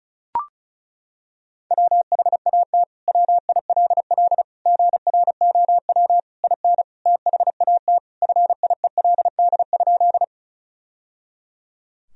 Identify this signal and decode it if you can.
Morse code, heard as a keyed tone, about 34 wpm, 700 Hz. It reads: WHAT WILL GROW IN THAT FIELD?
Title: Sentences from Top 1000 Words - Review Part 2 - 35wpm